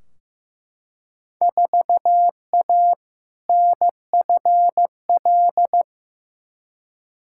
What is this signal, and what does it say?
Morse code, heard as a keyed tone, about 15 wpm, 700 Hz. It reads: 4A NFL
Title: Field Day Exchanges - 15wpm